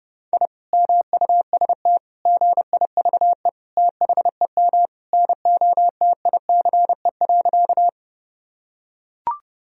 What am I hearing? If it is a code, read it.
Morse code, heard as a keyed tone, about 31 wpm, 700 Hz. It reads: I MUST GIVE THEM NOTICE.
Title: Sentences from Top 1000 Words - Review Part 2 - 30wpm